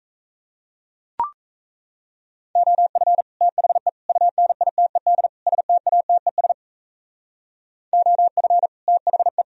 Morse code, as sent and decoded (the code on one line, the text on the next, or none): --- ..-. / - .... / ..- -. .. - . -.. / ... - .- - . ... / --- ..-. / - .... .
OF THE UNITED STATES OF THE